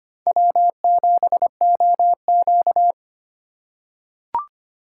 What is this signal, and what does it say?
Morse code, heard as a keyed tone, about 25 wpm, 700 Hz. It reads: W7OQ
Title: K1USN SST QSO Practice - 25wpm